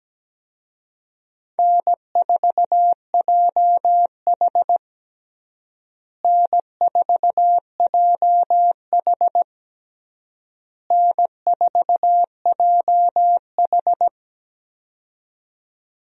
Morse code, heard as a keyed tone, about 17 wpm, 700 Hz. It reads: N4JH N4JH N4JH
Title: USA Call Signs - 4 Characters - Repeated 3x - 17wpm